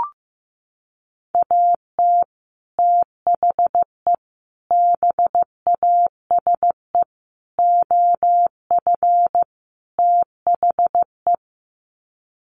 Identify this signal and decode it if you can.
Morse code, heard as a keyed tone, about 15 wpm, 700 Hz. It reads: AT THE BASE OF THE
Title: Sets of 5 Words 15wpm